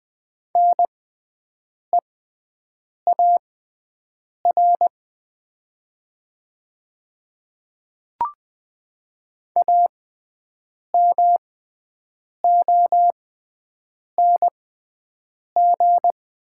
Morse code, heard as a keyed tone, about 21 wpm, spaced as for 7 wpm, 700 Hz. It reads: NEAR AMONG
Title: 6x Character Spacing - Top 500 Words - 20wpm